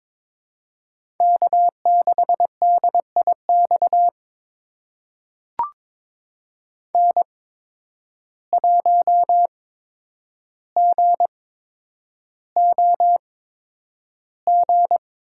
Morse code, heard as a keyed tone, about 22 wpm, spaced as for 9 wpm, 700 Hz. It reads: K6DIX N1GOG
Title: USA Call Signs - 5 Characters - 8x Character Spacing - 22wpm